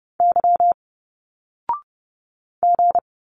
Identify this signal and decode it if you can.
Morse code, heard as a keyed tone, about 30 wpm, 700 Hz. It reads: Y G
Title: Single Letters 30wpm